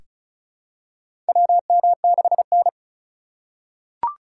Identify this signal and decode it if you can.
Morse code, heard as a keyed tone, about 36 wpm, 700 Hz. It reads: WM6N
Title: USA Call Signs - 4 Characters - 3x Character Spacing - 35wpm